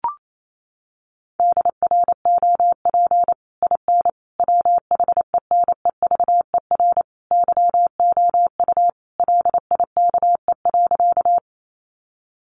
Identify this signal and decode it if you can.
Morse code, heard as a keyed tone, about 28 wpm, 700 Hz. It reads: DROP IN WHENEVER YOU LIKE.
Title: Sentences from Top 1400 Words 28wpm